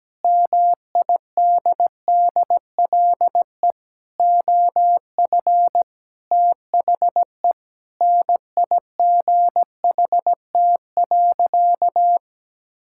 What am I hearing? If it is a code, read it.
Morse code, heard as a keyed tone, about 17 wpm, 700 Hz. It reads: MIDDLE OF THE NIGHT.
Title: Sentences from Top 1000 Words - Review Part 6 - 17wpm